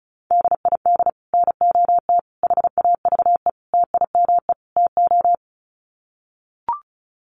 Morse code, heard as a keyed tone, about 34 wpm, 700 Hz. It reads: DID NOT HAVE TIME TO
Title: Sets of 5 Words Encore 35wpm